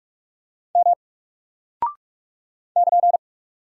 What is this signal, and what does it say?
Morse code, heard as a keyed tone, about 44 wpm, 700 Hz. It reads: M <KN>
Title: Single Character (Letter:Number:Punctuation:Prosign) 45wpm